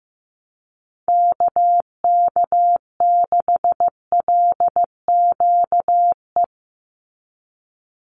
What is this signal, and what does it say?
Morse code, heard as a keyed tone, about 15 wpm, 700 Hz. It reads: KK6LQE
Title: USA Call Signs 15wpm